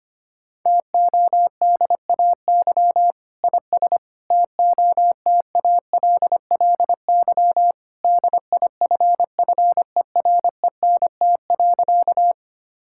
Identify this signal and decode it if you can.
Morse code, heard as a keyed tone, about 25 wpm, 700 Hz. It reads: TODAY IS TOTALLY DIFFERENT.
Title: Sentences from Top 1600 Words 25wpm